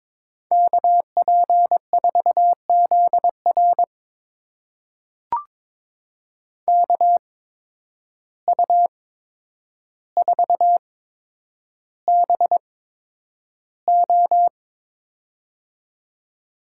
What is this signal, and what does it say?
Morse code, heard as a keyed tone, about 21 wpm, spaced as for 9 wpm, 700 Hz. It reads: KP4ZR KU4BO
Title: USA Call Signs - 5 Characters - 8x Character Spacing - 22wpm